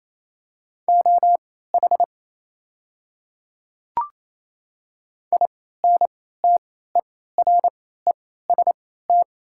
Morse code, heard as a keyed tone, about 28 wpm, spaced as for 16 wpm, 700 Hz. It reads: OH INTEREST
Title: 3x Character Spacing - Top 500 Words - 28wpm